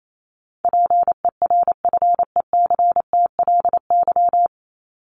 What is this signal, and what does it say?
Morse code, heard as a keyed tone, about 28 wpm, 700 Hz. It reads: PERFECTLY
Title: Top 1501-1750 Words 28wpm